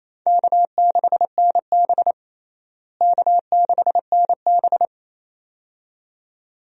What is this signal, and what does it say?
Morse code, heard as a keyed tone, about 28 wpm, 700 Hz. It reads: K6NB K6NB
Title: USA Call Signs - 4 Characters - Repeated 2x - 28wpm